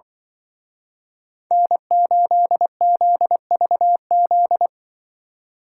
Morse code, heard as a keyed tone, about 23 wpm, 700 Hz. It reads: N8ZVZ
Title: Speed Racing - USA 1x3 Call Signs 20wpm